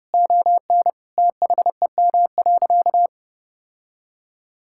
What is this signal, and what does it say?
Morse code, heard as a keyed tone, about 30 wpm, 700 Hz. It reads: ON THEM.
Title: Sentences from Top 1600 Words 30wpm